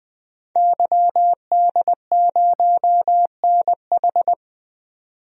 Morse code, heard as a keyed tone, about 20 wpm, 700 Hz. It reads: YD0NH
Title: DXCC-100 Call Signs - 5 Characters - 20wpm